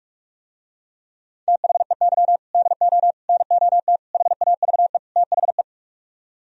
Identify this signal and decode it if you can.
Morse code, heard as a keyed tone, about 44 wpm, 700 Hz. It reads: THEY DO NOT HAVE THE